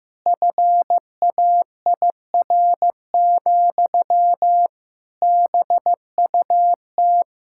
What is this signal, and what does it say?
Morse code, heard as a keyed tone, about 15 wpm, 700 Hz. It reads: FAIR, BUT
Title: Sentences from Top 1600 Words 15wpm